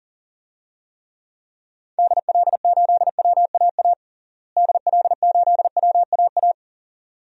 Morse code, heard as a keyed tone, about 40 wpm, 700 Hz. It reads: DL8WAA DL8WAA
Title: DXCC-100 Call Signs - 6 Characters - Repeated 2x - 40wpm